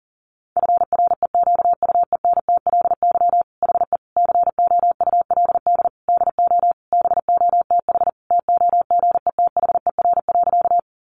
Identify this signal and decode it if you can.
Morse code, heard as a keyed tone, about 40 wpm, 700 Hz. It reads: FREQUENTLY HE COULD DO BOTH TOGETHER.
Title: Sentences from Top 1300 Words 40wpm